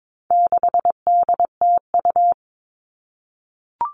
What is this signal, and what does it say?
Morse code, heard as a keyed tone, about 22 wpm, 700 Hz. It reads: 6DTU
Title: All Cut Numbers - 4-Digit 22wpm